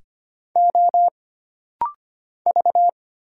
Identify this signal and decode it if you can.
Morse code, heard as a keyed tone, about 25 wpm, 700 Hz. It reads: O V
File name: Single Character (Letter:Number:Punctuation:Prosign) - Rapid-Fire - 25wpm